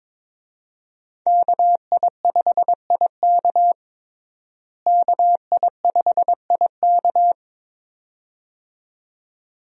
Morse code, heard as a keyed tone, about 22 wpm, 700 Hz. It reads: KI5IK KI5IK
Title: USA Call Signs - 5 Characters - Repeated 2x - 22wpm